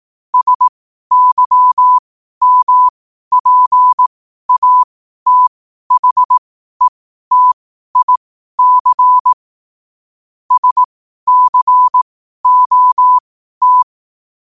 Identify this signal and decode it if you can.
Morse code, heard as a keyed tone, about 18 wpm, spaced as for 12 wpm, 1000 Hz. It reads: SYMPATHETIC SCOT
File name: qotdfiles / 12wpm